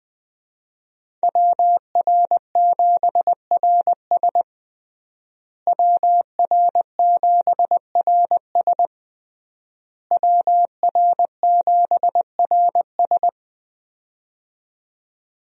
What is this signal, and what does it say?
Morse code, heard as a keyed tone, about 21 wpm, 700 Hz. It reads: WR7RS WR7RS WR7RS
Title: USA Call Signs - 5 Characters - Repeated 3x - 20wpm